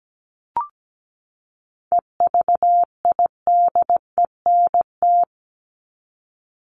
Morse code, heard as a keyed tone, about 17 wpm, 700 Hz. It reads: EVIDENT
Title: LY Suffix - Top 100 Words with Base Words 17wpm